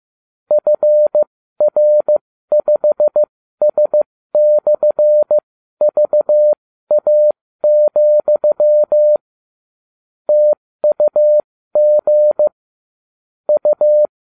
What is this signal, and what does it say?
Morse code, heard as a keyed tone, about 15 wpm, spaced as for 13 wpm, 600 Hz. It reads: FR5S/VA, TUG U